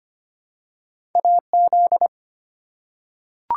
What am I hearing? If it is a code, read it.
Morse code, heard as a keyed tone, about 25 wpm, 700 Hz. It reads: AZ
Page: morse-code-ninja-practice-25wpm